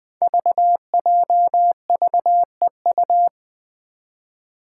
Morse code, heard as a keyed tone, about 20 wpm, 700 Hz. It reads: VJVEU
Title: Five Challenging Letter-Number Combinations 20wpm